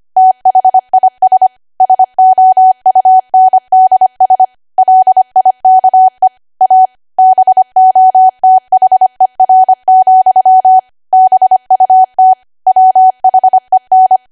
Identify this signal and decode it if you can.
Morse code, heard as a keyed tone, about 25 wpm, 750 Hz. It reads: THIS SOUNDS LIKE A BOTHER, BUT WHEN
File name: Archive / 25 WPM / 25WPM